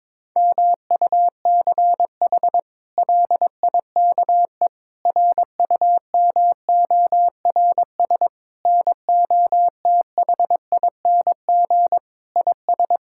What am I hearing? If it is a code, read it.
Morse code, heard as a keyed tone, about 22 wpm, 700 Hz. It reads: MUCH LIKE RUMORS NOTHING IS
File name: practice / 22wpm / Sets of 8 Words - 22wpm